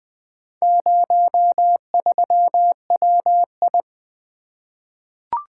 Morse code, heard as a keyed tone, about 20 wpm, 700 Hz. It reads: 03WI
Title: Four Letter-Number Combinations - 20wpm